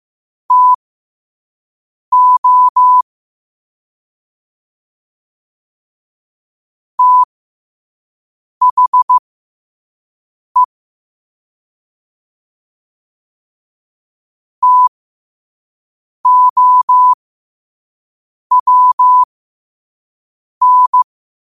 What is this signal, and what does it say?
Morse code, heard as a keyed tone, about 15 wpm, spaced as for 5 wpm, 1000 Hz. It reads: TO THE TOWN
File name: qotdfiles / 5wpm